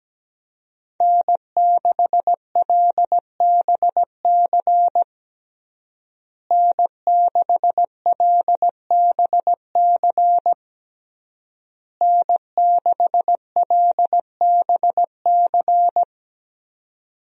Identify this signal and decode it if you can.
Morse code, heard as a keyed tone, about 17 wpm, 700 Hz. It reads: N6LBC N6LBC N6LBC
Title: USA Call Signs - 5 Characters - Repeated 3x - 17wpm